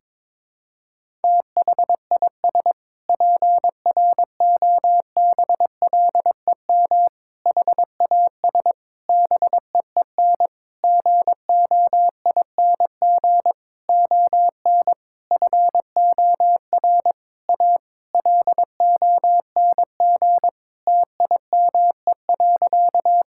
Morse code, heard as a keyed tone, about 22 wpm, 700 Hz. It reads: THIS PROBLEM HAS BEEN GOING ON FOR A LONG TIME.
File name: practice / 22wpm / Sentences with 8-Word Phrases - 22wpm